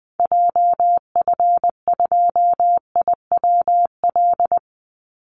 Morse code, heard as a keyed tone, about 20 wpm, 700 Hz. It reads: JF2IWL
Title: Call Signs Repeated 3x - 20wpm